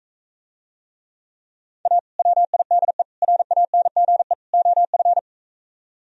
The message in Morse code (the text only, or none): A WIDE RANGE OF